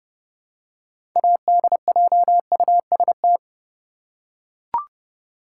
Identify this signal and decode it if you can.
Morse code, heard as a keyed tone, about 32 wpm, 700 Hz. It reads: ADJUST
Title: Six Letter Words - 30wpm